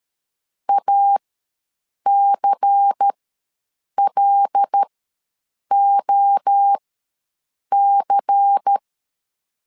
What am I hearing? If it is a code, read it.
Morse code, heard as a keyed tone, about 13 wpm, spaced as for 7 wpm, 800 Hz. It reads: ACLOC
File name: LEZIONI / 5 LBVC / 5 LBVC ws 12wpm 3159